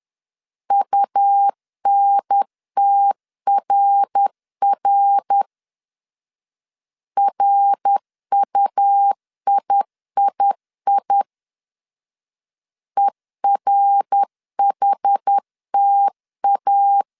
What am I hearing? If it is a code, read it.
Morse code, heard as a keyed tone, about 10 wpm, 800 Hz. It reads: UNTRR RUIII ERHTA
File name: CW / DET / 1 3 1500C S B 10wpm 3335